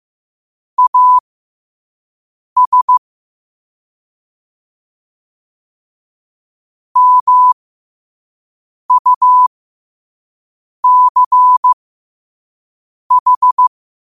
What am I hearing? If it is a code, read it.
Morse code, heard as a keyed tone, about 15 wpm, spaced as for 5 wpm, 1000 Hz. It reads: AS MUCH
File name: qotdfiles / 5wpm